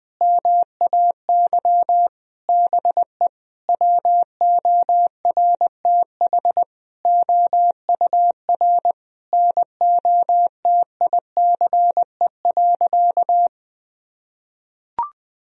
Morse code, heard as a keyed tone, about 20 wpm, 700 Hz. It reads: MAY BE WORTH OUR NOTICE.
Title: Sentences from Top 1000 Words - Review Part 2 - 20wpm